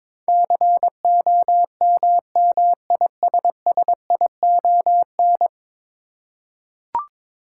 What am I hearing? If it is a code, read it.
Morse code, heard as a keyed tone, about 22 wpm, 700 Hz. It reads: COMMISSION
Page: morse-code-ninja-practice-22wpm